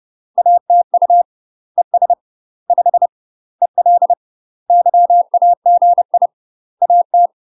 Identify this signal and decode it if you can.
Morse code, heard as a keyed tone, about 30 wpm, 700 Hz. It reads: ATU ES 5 EL YAGI AT
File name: UK MASTER QSOs TRAINING / 30WPM / UK MASTER QSOs 0007